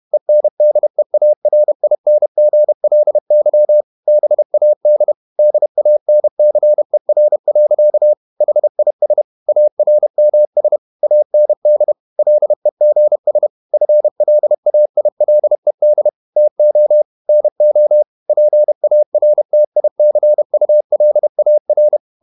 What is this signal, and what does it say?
Morse code, heard as a keyed tone, about 31 wpm, 600 Hz. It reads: ENDEARINGLY BAD DANCER. HIS ARMS AND LEGS FLAILED TO NO PARTICULAR
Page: Never 31wpm | CW med Gnister